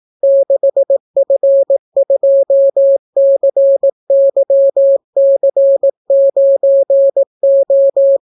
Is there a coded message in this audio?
Yes: Morse code from 0.2 to 8.2 s, heard as a keyed tone, about 18 wpm, 550 Hz. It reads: 6F2CYC9O